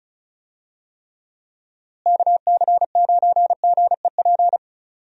Morse code, heard as a keyed tone, about 36 wpm, 700 Hz. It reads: KC9GEP
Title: USA Call Signs - 6 Characters - Repeated 2x - 35wpm